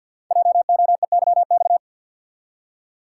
Morse code, heard as a keyed tone, about 48 wpm, 700 Hz. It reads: JOEYX
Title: Code Groups - 5 Letter:Numbers - 50wpm